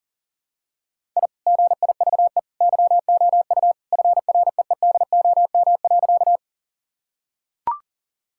Morse code, heard as a keyed tone, about 40 wpm, 700 Hz. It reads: I GIVE YOU FREEDOM.
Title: Sentences from Top 1000 Words - Review Part 5 - 40wpm